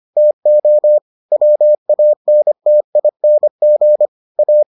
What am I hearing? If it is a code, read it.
Morse code, heard as a keyed tone, about 25 wpm, 600 Hz. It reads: TO WANTING A